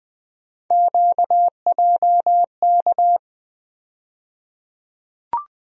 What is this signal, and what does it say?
Morse code, heard as a keyed tone, about 20 wpm, 700 Hz. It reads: QJK
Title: USA Call Sign Postfixes - 3 Letters 20wpm